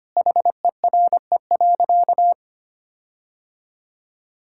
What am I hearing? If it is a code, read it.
Morse code, heard as a keyed tone, about 25 wpm, 700 Hz. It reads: HERE.
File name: practice / 25wpm / Sentences with 4-Word Phrases - 25wpm